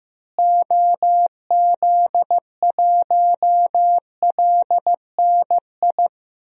Morse code, heard as a keyed tone, about 15 wpm, 700 Hz. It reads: OZ1LNI
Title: DXCC-100 Call Signs - 6 Characters - Repeated 3x - 15wpm